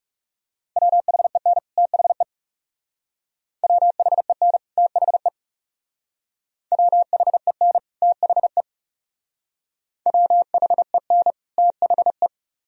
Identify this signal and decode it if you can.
Morse code, heard as a keyed tone, about 38 wpm, 700 Hz. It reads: WHEN THE WHEN THE WHEN THE WHEN THE